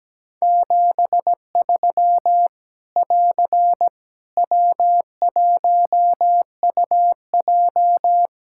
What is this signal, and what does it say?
Morse code, heard as a keyed tone, about 17 wpm, 700 Hz. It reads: 73 <AR> W1UJ
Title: Practice QSO - Exchange 4 - Ending - 17wpm